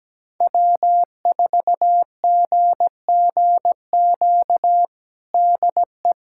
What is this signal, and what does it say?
Morse code, heard as a keyed tone, about 17 wpm, 700 Hz. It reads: W4GGQ DE
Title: USA Callsign DE Callsign - 5 Characters - 17wpm